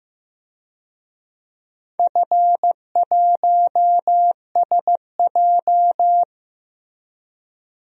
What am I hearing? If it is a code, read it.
Morse code, heard as a keyed tone, about 15 wpm, 700 Hz. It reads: F1SJ
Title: DXCC-100 Call Signs - 4 Characters - 15wpm